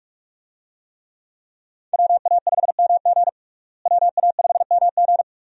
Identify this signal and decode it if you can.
Morse code, heard as a keyed tone, about 41 wpm, 700 Hz. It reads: WA5MG WA5MG
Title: USA Call Signs - 5 Characters - Repeated 2x - 45wpm